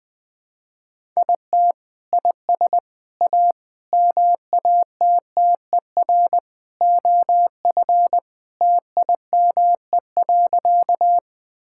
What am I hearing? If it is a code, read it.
Morse code, heard as a keyed tone, about 21 wpm, 700 Hz. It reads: IT IS A MATTER OF TIME.
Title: Sentences with 3-Word Phrases - 20wpm